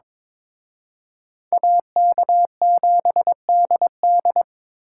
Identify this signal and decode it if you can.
Morse code, heard as a keyed tone, about 26 wpm, 700 Hz. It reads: AK7DD
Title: Speed Racing - USA 2x2 Call Signs 22wpm